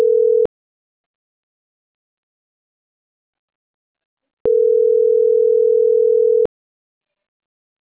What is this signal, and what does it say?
Ringback tone (North America), 440 + 480 Hz.